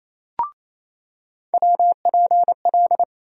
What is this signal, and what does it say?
Morse code, heard as a keyed tone, about 28 wpm, 700 Hz. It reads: WPL
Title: USA Call Sign Postfixes - 3 Letters 28wpm